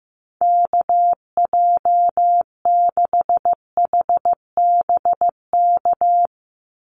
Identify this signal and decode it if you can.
Morse code, heard as a keyed tone, about 15 wpm, 700 Hz. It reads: KJ6HBK
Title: USA Call Signs 15wpm